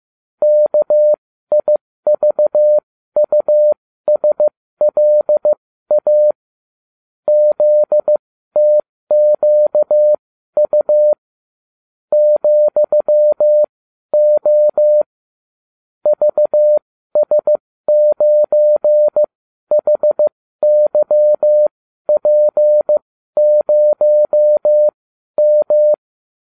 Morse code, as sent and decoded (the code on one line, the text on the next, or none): -.- .. ...- ..- ... .-.. .- / --.. - --.- ..- / --..-- --- / ...- ... ----. .... -.-- .--. ----- --
KIVUSLA ZTQU ,O VS9HYP0M